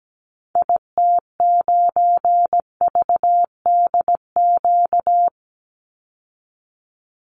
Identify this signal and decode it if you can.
Morse code, heard as a keyed tone, about 17 wpm, 700 Hz. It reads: IT9VDQ
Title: Call Signs - 17wpm